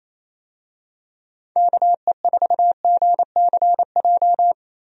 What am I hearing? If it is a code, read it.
Morse code, heard as a keyed tone, about 27 wpm, 700 Hz. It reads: KE4GCJ
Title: USA Call Signs - 6 Characters - Repeated 3x - 28wpm